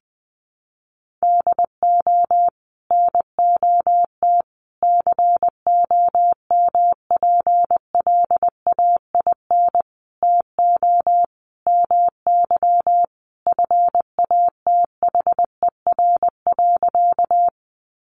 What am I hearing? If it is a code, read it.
Morse code, heard as a keyed tone, about 20 wpm, 700 Hz. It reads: DO NOT COMPLAIN TO MY FATHER.
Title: Sentences from Top 1300 Words 20wpm